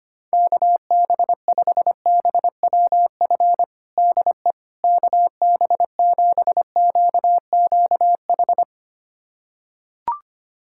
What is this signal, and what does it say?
Morse code, heard as a keyed tone, about 25 wpm, 700 Hz. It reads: KB5BWF DE KB7QQH
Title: USA Callsign DE Callsign 25wpm